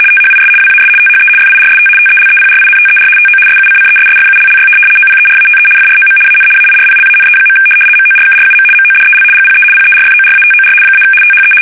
RTTY, about 75 baud. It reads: THIS WAS THE FIRST TELEPRINTER TO SUCCESSFULLY OPERATE FROM AN AIRPLANE. MORKRUM MODEL GPE PERFORATOR - THE
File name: fsk850-75B / teletype-corp